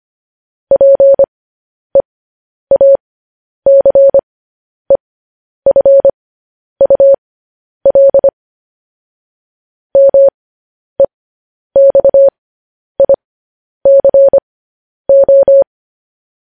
Morse code, heard as a keyed tone, about 25 wpm, spaced as for 10 wpm, 550 Hz. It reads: PEACEFUL MEXICO
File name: podcast / News headlines-1x-25x10